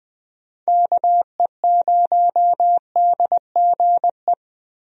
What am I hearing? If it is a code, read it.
Morse code, heard as a keyed tone, about 20 wpm, 700 Hz. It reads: KE0DGE
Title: USA Call Signs - 6 Characters - Repeated 2x - 20wpm